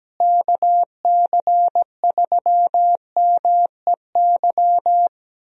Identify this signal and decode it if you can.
Morse code, heard as a keyed tone, about 17 wpm, 700 Hz. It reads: KC3MEY
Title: USA Call Signs - 6 Characters - Repeated 3x - 17wpm